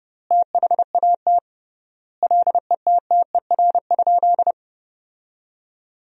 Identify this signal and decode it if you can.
Morse code, heard as a keyed tone, about 30 wpm, 700 Hz. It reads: THAT LETTER?
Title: 3x Word-Spacing - Sentences from Top 300 Words - 30wpm